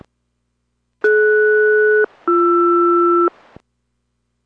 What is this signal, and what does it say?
Aircraft SELCAL, heard as two pairs of tones. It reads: DS-BR